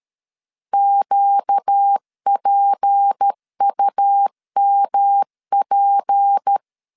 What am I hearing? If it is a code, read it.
Morse code, heard as a keyed tone, about 13 wpm, 800 Hz. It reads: QPUMP
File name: LEZIONI / 6 PKX0 / 6 P 12wpm 2033